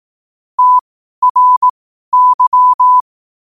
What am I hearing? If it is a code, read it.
Morse code, heard as a keyed tone, about 18 wpm, spaced as for 12 wpm, 1000 Hz. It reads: TRY
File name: qotdfiles / 12wpm